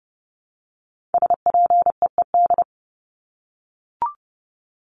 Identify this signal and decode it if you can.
Morse code, heard as a keyed tone, about 30 wpm, 700 Hz. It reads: SPEED
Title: Top 751-1000 Words 30wpm